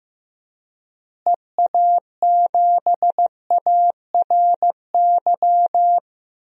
Morse code, heard as a keyed tone, about 15 wpm, 700 Hz. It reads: EA7ARY
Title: DXCC-100 Call Signs - 6 Characters - Repeated 3x - 15wpm